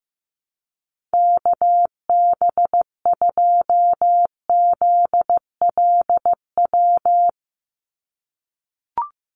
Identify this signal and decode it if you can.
Morse code, heard as a keyed tone, about 15 wpm, 700 Hz. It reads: KB2ZLW
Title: USA Call Signs 15wpm